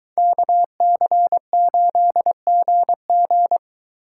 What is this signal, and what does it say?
Morse code, heard as a keyed tone, about 23 wpm, 700 Hz. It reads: KC8GG